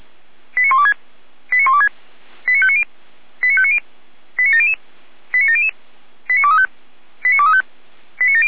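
Five-tone selective call (ZVEI), heard as quick runs of five tones, tones of about 70 ms. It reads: 89126 89126 89590 89590 89700 89700 89235 89235
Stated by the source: ZVEI five-tone selective call received on SDR.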